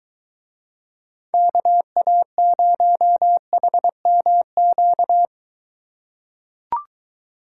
Morse code, heard as a keyed tone, about 22 wpm, 700 Hz. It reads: KA0HMQ